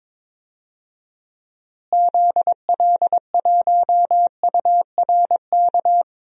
Morse code, heard as a keyed tone, about 22 wpm, 700 Hz. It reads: ZL1URK
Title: DXCC-100 Call Signs - 6 Characters - Repeated 3x - 22wpm